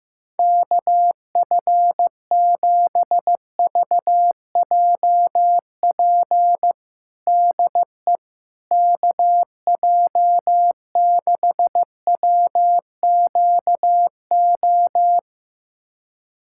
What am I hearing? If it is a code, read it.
Morse code, heard as a keyed tone, about 15 wpm, 700 Hz. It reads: KF7VJP DE KJ6WQO
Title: USA Callsign DE Callsign 15wpm